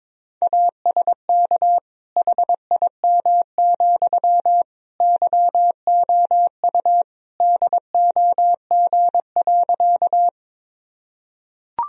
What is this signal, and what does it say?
Morse code, heard as a keyed tone, about 22 wpm, 700 Hz. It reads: ASK HIM, YOU DOG.
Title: Sentences from Top 1000 Words - Review Part 2 - 22wpm